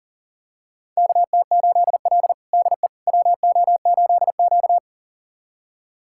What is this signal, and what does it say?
Morse code, heard as a keyed tone, about 40 wpm, 700 Hz. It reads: KT8L DE WO8Q
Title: USA Callsign DE Callsign - 4 Characters - 40wpm